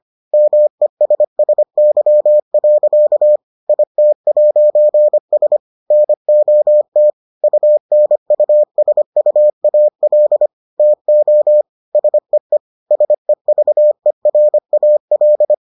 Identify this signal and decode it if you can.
Morse code, heard as a keyed tone, about 25 wpm, 600 Hz. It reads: MESSY. IT'S NOT UNUSUAL TO SEE SEVERAL